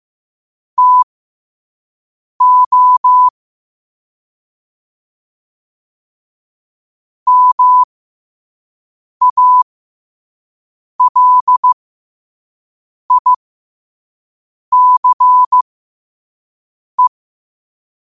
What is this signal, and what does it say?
Morse code, heard as a keyed tone, about 15 wpm, spaced as for 5 wpm, 1000 Hz. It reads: TO MALICE